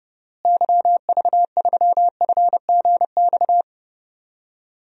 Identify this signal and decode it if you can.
Morse code, heard as a keyed tone, about 30 wpm, 700 Hz. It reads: YV3FGX
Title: DXCC-100 Call Signs - 6 Characters - Repeated 2x - 30wpm